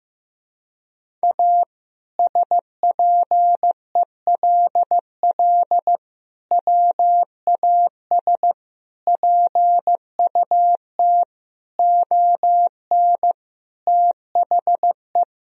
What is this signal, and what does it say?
Morse code, heard as a keyed tone, about 15 wpm, 700 Hz. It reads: A SPELL WAS PUT ON THE
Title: Sentences from Top 1000 Words - Review Part 1 - 15wpm